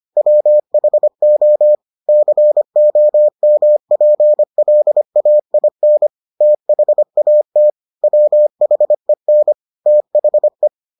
Morse code, heard as a keyed tone, about 25 wpm, 600 Hz. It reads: WHO COMPLAIN THAT WHEN THE